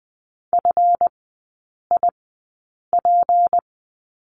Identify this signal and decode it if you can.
Morse code, heard as a keyed tone, about 20 wpm, spaced as for 8 wpm, 700 Hz. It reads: FIP
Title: 218 Lesson 35e - Number 0 - Three Characters 20wpm